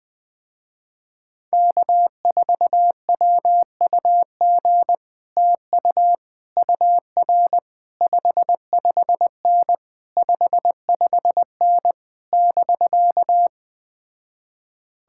Morse code, heard as a keyed tone, about 20 wpm, 700 Hz. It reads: K4WUG TU UR 55N 55N <BK>
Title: Speed-Racing Common QSO Phrases - 17wpm